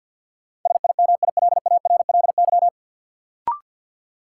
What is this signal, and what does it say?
Morse code, heard as a keyed tone, about 52 wpm, 700 Hz. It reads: SIMILARLY
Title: Sentences from Top 1400 Words - New Word Drill - 50wpm